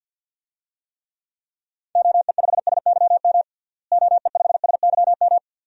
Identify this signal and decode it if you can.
Morse code, heard as a keyed tone, about 47 wpm, 700 Hz. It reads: OE5SYM OE5SYM
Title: DXCC-100 Call Signs - 6 Characters - Repeated 2x - 50wpm